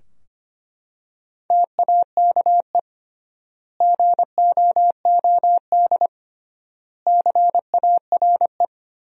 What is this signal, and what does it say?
Morse code, heard as a keyed tone, about 25 wpm, 700 Hz. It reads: TAKE GOOD CARE
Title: 3x Word-Spacing - Sentences from Top 400 Words - 25wpm